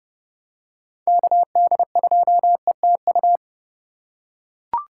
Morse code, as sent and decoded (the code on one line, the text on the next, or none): -.- -.. ..--- . - ..-
KD2ETU